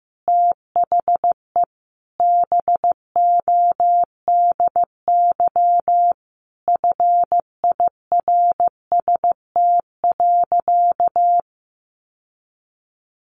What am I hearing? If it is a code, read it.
Morse code, heard as a keyed tone, about 14 wpm, 700 Hz. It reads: THE BODY FIRST.
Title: Sentences from Top 1400 Words 15wpm